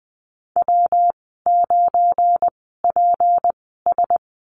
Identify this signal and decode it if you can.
Morse code, heard as a keyed tone, about 20 wpm, spaced as for 14 wpm, 700 Hz. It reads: W9PS
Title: 186 Lesson 30g - Letter b - Call Signs 20wpm